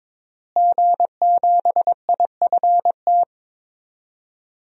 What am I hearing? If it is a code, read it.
Morse code, heard as a keyed tone, about 22 wpm, 700 Hz. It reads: G7IFT
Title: DXCC-100 Call Signs - 5 Characters - Repeated 3x - 22wpm